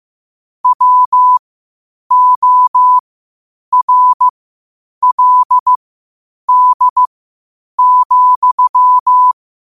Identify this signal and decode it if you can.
Morse code, heard as a keyed tone, about 15 wpm, spaced as for 8 wpm, 1000 Hz. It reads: WORLD,
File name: 8wpm